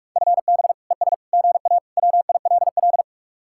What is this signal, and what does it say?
Morse code, heard as a keyed tone, about 46 wpm, 700 Hz. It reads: FB ES GA WILL